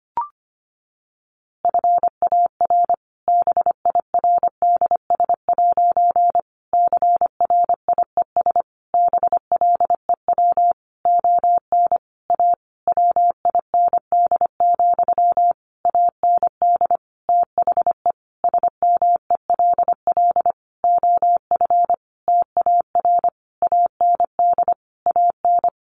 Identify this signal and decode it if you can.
Morse code, heard as a keyed tone, about 25 wpm, 700 Hz. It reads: FAR BIRDS' CRIES BLEW ON A WIND, AND THE SMELL OF TAR AND AN
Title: A Sound Of Thunder 25wpm